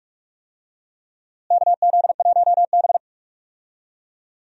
Morse code, heard as a keyed tone, about 46 wpm, 700 Hz. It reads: KZ1B
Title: USA Call Signs - 4 Characters - Repeated 3x - 45wpm